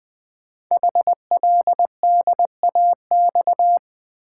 Morse code, heard as a keyed tone, about 20 wpm, 700 Hz. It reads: HLDAX
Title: Code Groups - 5 Letters - 20wpm